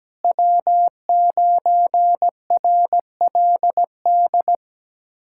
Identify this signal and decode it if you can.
Morse code, heard as a keyed tone, about 17 wpm, 700 Hz. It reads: W9RLD
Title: USA Callsign DE Callsign 17wpm